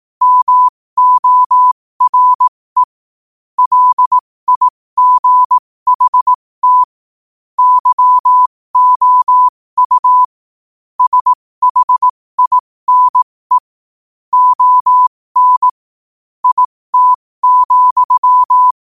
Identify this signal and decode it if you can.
Morse code, heard as a keyed tone, about 18 wpm, spaced as for 15 wpm, 1000 Hz. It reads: MORE LIGHT YOU SHINE ON IT,
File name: qotdfiles / 15wpm